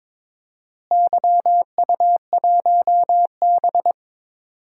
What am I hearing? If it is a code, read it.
Morse code, heard as a keyed tone, about 22 wpm, 700 Hz. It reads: YU1B
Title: DXCC-100 Call Signs - 4 Characters - Repeated 3x - 22wpm